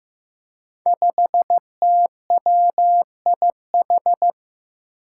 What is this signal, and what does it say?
Morse code, heard as a keyed tone, about 15 wpm, 700 Hz. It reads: 5TWIH
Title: Five Challenging Letter-Number Combinations 15wpm